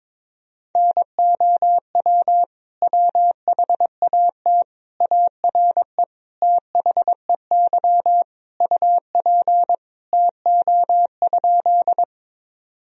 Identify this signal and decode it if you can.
Morse code, heard as a keyed tone, about 22 wpm, 700 Hz. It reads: NOW WHAT ARE THEY UP TO?
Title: Sentences from Top 1000 Words - Review Part 1 - 22wpm